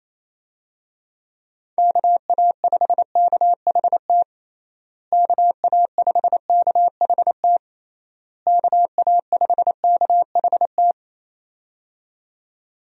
Morse code, heard as a keyed tone, about 27 wpm, 700 Hz. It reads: KA5KHT KA5KHT KA5KHT
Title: USA Call Signs - 6 Characters - Repeated 3x - 28wpm